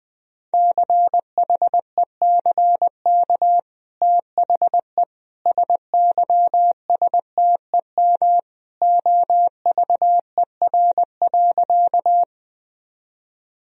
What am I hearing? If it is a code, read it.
Morse code, heard as a keyed tone, about 21 wpm, 700 Hz. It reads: CHECK THE SYSTEM OVER.
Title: Sentences from Top 1000 Words - Review Part 3 - 20wpm